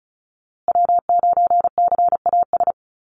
Morse code, heard as a keyed tone, about 34 wpm, 700 Hz. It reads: W9CAS
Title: USA Call Signs Repeated 2x 35wpm